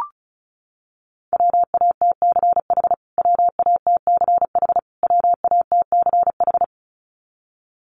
Morse code, heard as a keyed tone, about 34 wpm, 700 Hz. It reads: WATCH WATCH WATCH
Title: Top 500 Words Repeated 35wpm